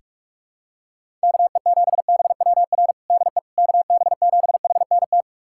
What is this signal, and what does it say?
Morse code, heard as a keyed tone, about 46 wpm, 700 Hz. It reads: KE7BWR DE KB7HNT
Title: USA Callsign DE Callsign - 6 Characters - 45wpm